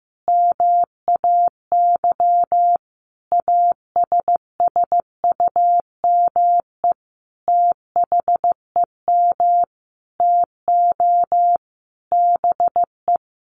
Morse code, heard as a keyed tone, about 15 wpm, 700 Hz. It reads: MAY ASSUME THEM TO BE
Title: Sentences from Top 1500 Words 15wpm